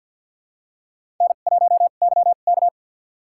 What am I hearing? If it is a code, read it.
Morse code, heard as a keyed tone, about 51 wpm, spaced as for 36 wpm, 700 Hz. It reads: N1YK
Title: USA Call Signs - 2x Character Spacing - 50wpm